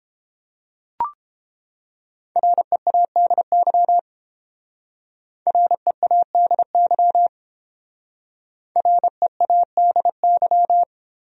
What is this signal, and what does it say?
Morse code, heard as a keyed tone, about 28 wpm, 700 Hz. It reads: READY READY READY